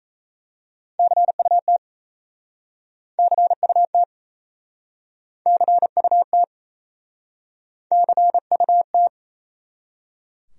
Morse code, heard as a keyed tone, about 39 wpm, 700 Hz. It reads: CUT CUT CUT CUT